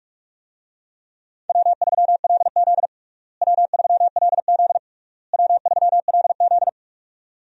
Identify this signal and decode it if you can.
Morse code, heard as a keyed tone, about 42 wpm, 700 Hz. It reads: W3LZ W3LZ W3LZ
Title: USA Call Signs - 4 Characters - Repeated 3x - 45wpm